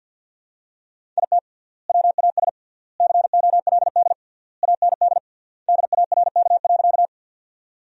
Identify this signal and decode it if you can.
Morse code, heard as a keyed tone, about 47 wpm, 700 Hz. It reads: IT WAS COLD AND DARK.
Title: 3x Word-Spacing - Sentences from Top 500 Words - 50wpm